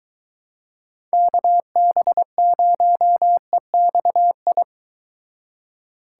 Morse code, heard as a keyed tone, about 24 wpm, 700 Hz. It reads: KB0EXI